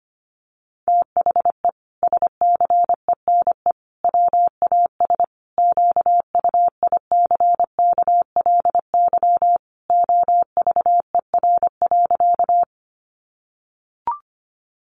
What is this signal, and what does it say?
Morse code, heard as a keyed tone, about 25 wpm, 700 Hz. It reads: THE SCENE WAS QUICKLY OVER.
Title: Sentences from Top 1100 Words 25wpm